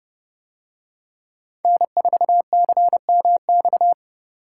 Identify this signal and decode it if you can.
Morse code, heard as a keyed tone, about 30 wpm, 700 Hz. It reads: N4CMX